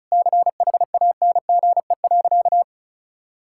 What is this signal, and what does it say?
Morse code, heard as a keyed tone, about 34 wpm, 700 Hz. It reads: CHANGE.
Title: Sentences from Top 1000 Words - Review Part 6 - 35wpm